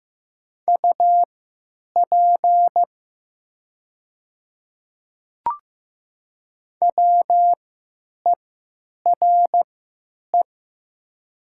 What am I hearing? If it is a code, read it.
Morse code, heard as a keyed tone, about 15 wpm, spaced as for 9 wpm, 700 Hz. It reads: UP WERE
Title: 3x Character Spacing - Top 100 Words - 15wpm